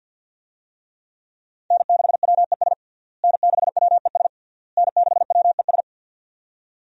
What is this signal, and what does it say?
Morse code, heard as a keyed tone, about 49 wpm, 700 Hz. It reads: N6WES N6WES N6WES
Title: USA Call Signs - 5 Characters - Repeated 3x - 50wpm